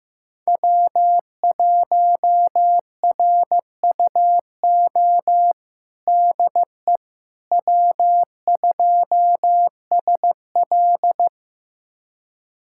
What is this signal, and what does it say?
Morse code, heard as a keyed tone, about 15 wpm, 700 Hz. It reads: W1RUO DE W2SL
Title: USA Callsign DE Callsign 15wpm